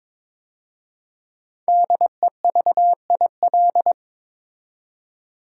Morse code, heard as a keyed tone, about 22 wpm, 700 Hz. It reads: DEVIL
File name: 5 Letter Words - First 3 Letters Given - 22wpm